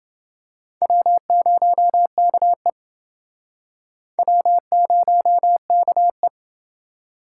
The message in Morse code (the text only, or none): W0KE W0KE